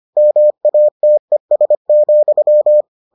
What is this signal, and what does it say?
Morse code, heard as a keyed tone, about 25 wpm, 600 Hz. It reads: MATES,